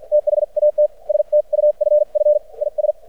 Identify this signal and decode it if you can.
Morse code, heard as a keyed tone, about 52 wpm, spaced as for 43 wpm, 600 Hz. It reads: THAT STUFF IS
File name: archived / CRSnet